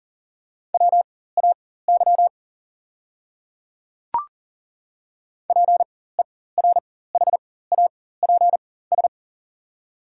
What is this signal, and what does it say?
Morse code, heard as a keyed tone, about 40 wpm, spaced as for 19 wpm, 690 Hz. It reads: WAY PERHAPS
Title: 4x Character Spacing - Top 500 Words - 40wpm